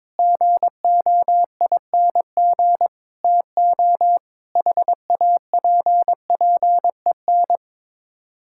Morse code, heard as a keyed tone, about 22 wpm, 700 Hz. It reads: GOING TO HAPPEN
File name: practice / 22wpm / Sets of 7 Words - 22wpm